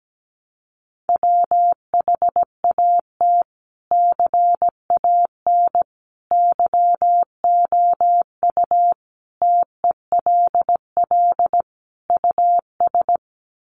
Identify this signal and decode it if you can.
Morse code, heard as a keyed tone, about 17 wpm, 700 Hz. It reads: WHAT CAN YOU TELL US
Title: Sets of 5 Words Encore 17wpm